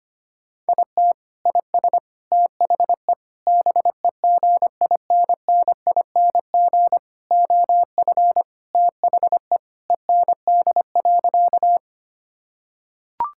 Morse code, heard as a keyed tone, about 25 wpm, 700 Hz. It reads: IT IS THE BEGINNING OF THE END.
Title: Sentences from Top 1000 Words - Review Part 5 - 25wpm